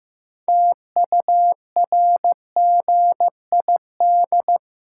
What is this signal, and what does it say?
Morse code, heard as a keyed tone, about 15 wpm, 700 Hz. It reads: TURGID
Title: Pangram Challenge v2 - 15wpm